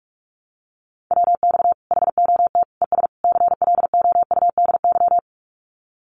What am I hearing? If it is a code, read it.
Morse code, heard as a keyed tone, about 44 wpm, 700 Hz. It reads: WX HOT ES CLOUDY
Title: Common QSO Phrases 45wpm